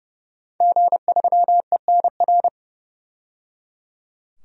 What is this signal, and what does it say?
Morse code, heard as a keyed tone, about 29 wpm, 700 Hz. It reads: G3ENR
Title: DXCC-100 Call Signs - 5 Characters - 30wpm